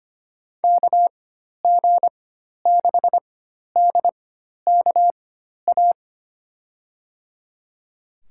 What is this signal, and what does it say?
Morse code, heard as a keyed tone, about 25 wpm, spaced as for 12 wpm, 700 Hz. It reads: KG6DKA
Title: USA Call Signs - 4x Character Spacing - 25wpm